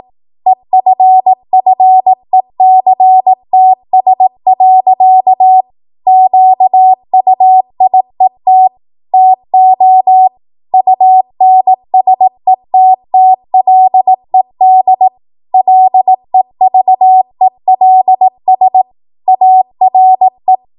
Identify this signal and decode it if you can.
Morse code, heard as a keyed tone, about 18 wpm, 750 Hz. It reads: EFFECTS. QUIET TO UNSETTLED LEVELS ARE
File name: Morse / 18WPM